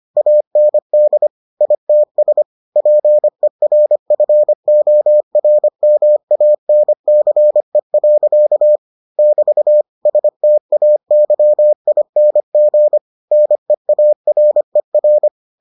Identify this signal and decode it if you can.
Morse code, heard as a keyed tone, about 25 wpm, 600 Hz. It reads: AND ITS PERFORMANCE. <BT> STAYING NEARER